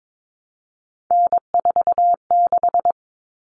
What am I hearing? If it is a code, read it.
Morse code, heard as a keyed tone, about 22 wpm, 700 Hz. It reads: N46
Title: Common Cut Numbers - 3-Digit 22wpm